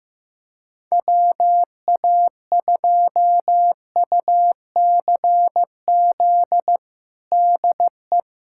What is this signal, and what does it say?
Morse code, heard as a keyed tone, about 16 wpm, 700 Hz. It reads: WA2UCZ DE
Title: USA Callsign DE Callsign - 6 Characters - 15wpm